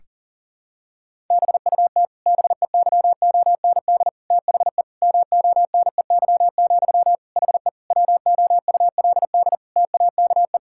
Morse code, heard as a keyed tone, about 40 wpm, 700 Hz. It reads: BUT BEYOND THE MONEY, HE WOULD TAKE
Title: Sentences from Top 1000 Words - Review Part 6 - 40wpm